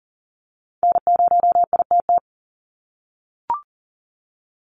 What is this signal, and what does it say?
Morse code, heard as a keyed tone, about 40 wpm, 700 Hz. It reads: N0ITT
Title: USA Call Signs 40wpm